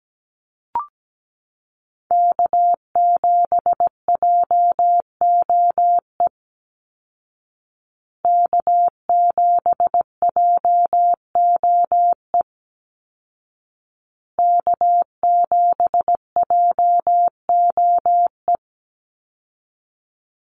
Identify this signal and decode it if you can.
Morse code, heard as a keyed tone, about 17 wpm, 700 Hz. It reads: K7JOE K7JOE K7JOE
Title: Call Signs Repeated 2x - 17wpm